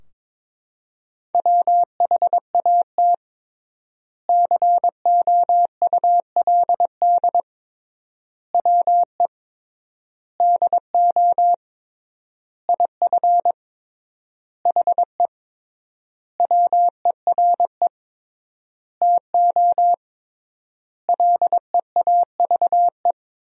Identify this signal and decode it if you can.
Morse code, heard as a keyed tone, about 22 wpm, 700 Hz. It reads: WHAT COULD WE DO IF HE WERE TO LEAVE
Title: 3x Word-Spacing - Sentences from Top 400 Words - 22wpm